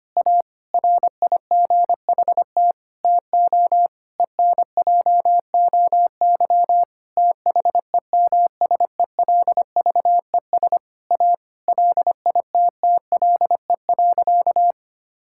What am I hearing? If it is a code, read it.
Morse code, heard as a keyed tone, about 25 wpm, 700 Hz. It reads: A RIGHT TO ENJOY THEMSELVES A LITTLE.
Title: Sentences from Top 1000 Words - Review Part 5 - 25wpm